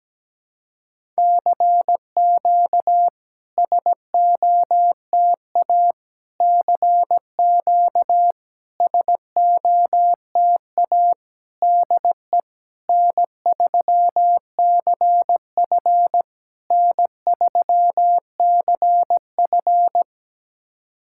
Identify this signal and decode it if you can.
Morse code, heard as a keyed tone, about 17 wpm, 700 Hz. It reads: CQ SOTA CQ SOTA DE N3CF N3CF
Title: SOTA QSO Practice 17wpm